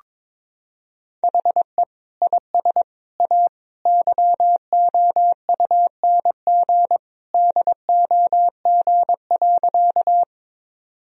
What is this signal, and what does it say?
Morse code, heard as a keyed tone, about 22 wpm, 700 Hz. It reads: HE IS A YOUNG DOG.
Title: Sentences from Top 1000 Words - Review Part 2 - 22wpm